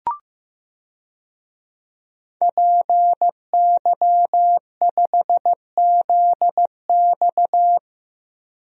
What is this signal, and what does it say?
Morse code, heard as a keyed tone, about 16 wpm, 700 Hz. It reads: PY5ZX
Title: DXCC-100 Call Signs - 5 Characters - 15wpm